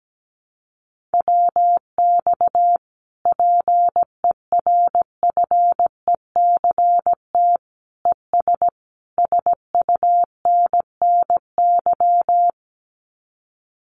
Morse code, heard as a keyed tone, about 17 wpm, 700 Hz. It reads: WX PERFECT ES SUNNY
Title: Common QSO Phrases Encore 17wpm